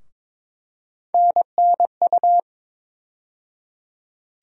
Morse code, heard as a keyed tone, about 22 wpm, 700 Hz. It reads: NNU
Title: USA Call Sign Postfixes - 3 Letters 22wpm